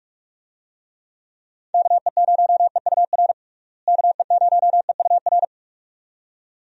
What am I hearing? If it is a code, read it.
Morse code, heard as a keyed tone, about 43 wpm, 700 Hz. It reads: KE0EUR KE0EUR
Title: USA Call Signs - 6 Characters - Repeated 2x - 45wpm